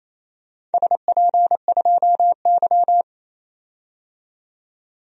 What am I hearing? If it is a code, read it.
Morse code, heard as a keyed tone, about 28 wpm, 700 Hz. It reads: SP2Y
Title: DXCC-100 Call Signs - 4 Characters - 28wpm